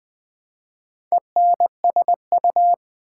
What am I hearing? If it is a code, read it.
Morse code, heard as a keyed tone, about 20 wpm, 700 Hz. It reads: ENSU
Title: Build A Word - 1 Letter At A Time - 5-Letter Words - 20wpm